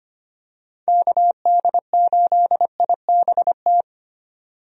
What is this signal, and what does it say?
Morse code, heard as a keyed tone, about 25 wpm, 700 Hz. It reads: KD8IBT
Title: USA Call Signs - 6 Characters - 25wpm